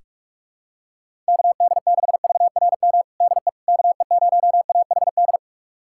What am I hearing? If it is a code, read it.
Morse code, heard as a keyed tone, about 46 wpm, 700 Hz. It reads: KD6VRM DE KE0AHD
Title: USA Callsign DE Callsign 45wpm